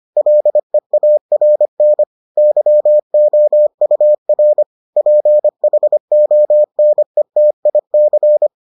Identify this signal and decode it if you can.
Morse code, heard as a keyed tone, about 25 wpm, 600 Hz. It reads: LEARN YOUR PHONETIC